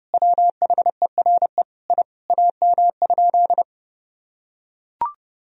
Morse code, heard as a keyed tone, about 32 wpm, 700 Hz. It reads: WHERE I AM?
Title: Sentences from Top 1000 Words - Review Part 3 - 30wpm